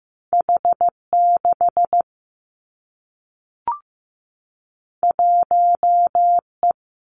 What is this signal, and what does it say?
Morse code, heard as a keyed tone, about 15 wpm, 700 Hz. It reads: H 6 1 E
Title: Two Challenging Letter-Number Combinations 15wpm